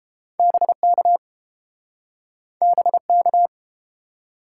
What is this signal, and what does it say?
Morse code, heard as a keyed tone, about 30 wpm, 700 Hz. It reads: BK BK